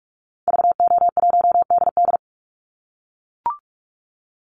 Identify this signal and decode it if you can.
Morse code, heard as a keyed tone, about 44 wpm, 700 Hz. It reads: VO1DD
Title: Call Signs - 45wpm